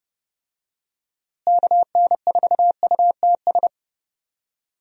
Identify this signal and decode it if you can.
Morse code, heard as a keyed tone, about 30 wpm, 700 Hz. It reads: KN4UTS
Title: USA Call Signs - 6 Characters - Repeated 3x - 30wpm